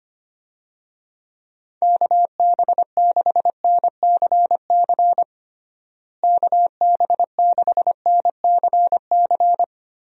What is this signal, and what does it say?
Morse code, heard as a keyed tone, about 25 wpm, 700 Hz. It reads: KB6NCC KB6NCC
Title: USA Call Signs - 6 Characters - Repeated 2x - 25wpm